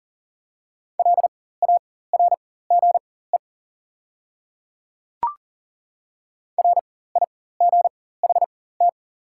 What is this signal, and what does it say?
Morse code, heard as a keyed tone, about 41 wpm, spaced as for 19 wpm, 700 Hz. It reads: LARGE RIGHT
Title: 4x Character Spacing - Top 200 Words - 40wpm